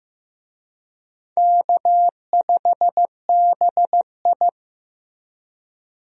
Morse code, heard as a keyed tone, about 15 wpm, 700 Hz. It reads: K5BI
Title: USA Call Signs - 4 Characters - Repeated 2x - 15wpm